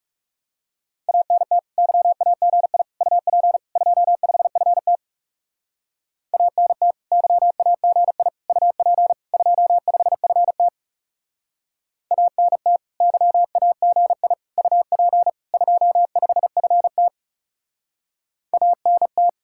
Morse code, heard as a keyed tone, about 40 wpm, 700 Hz. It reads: ANT YAGI UP 25FT ANT YAGI UP 25FT ANT YAGI UP 25FT ANT